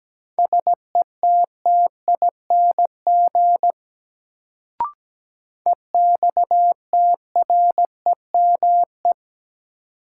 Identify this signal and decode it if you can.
Morse code, heard as a keyed tone, about 17 wpm, 700 Hz. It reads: SETTING EXTREME
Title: Sentences from Top 1300 Words - New Word Drill - 17wpm